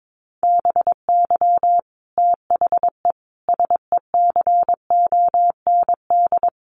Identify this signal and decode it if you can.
Morse code, heard as a keyed tone, about 22 wpm, 700 Hz. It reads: BY THE SECOND
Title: Sentences from Top 1200 Words 22wpm